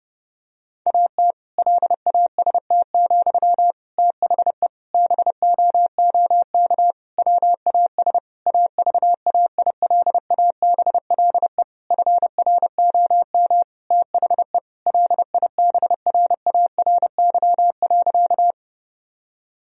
Morse code, heard as a keyed tone, about 30 wpm, 700 Hz. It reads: AT LAST, THE BOOK WAS AVAILABLE FROM THE LIBRARY.
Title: Sentences from Top 1000 Words - Review Part 3 - 30wpm